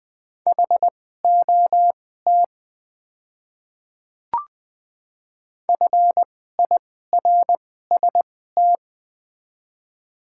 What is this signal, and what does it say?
Morse code, heard as a keyed tone, about 20 wpm, spaced as for 14 wpm, 700 Hz. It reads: HOT FIRST
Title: 2x Character Spacing - Top 100 Words - 20wpm